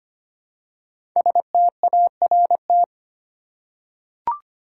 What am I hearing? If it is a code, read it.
Morse code, heard as a keyed tone, about 25 wpm, 700 Hz. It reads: START
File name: 5 Letter Words - First 3 Letters Given - 25wpm